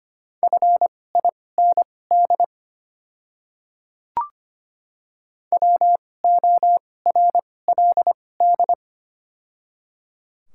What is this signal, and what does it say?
Morse code, heard as a keyed tone, about 25 wpm, spaced as for 18 wpm, 700 Hz. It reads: FIND WORLD
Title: 2x Character Spacing - Top 300 Words - 25wpm